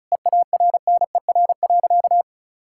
Morse code, heard as a keyed tone, about 36 wpm, 700 Hz. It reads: EARNER.
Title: Pangram Challenge v2 - 35wpm